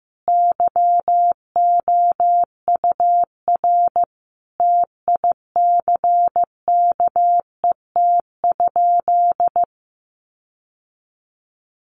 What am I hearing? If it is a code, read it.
Morse code, heard as a keyed tone, about 15 wpm, 700 Hz. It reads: YOUR TICKET?
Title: Sentences from Top 1400 Words 15wpm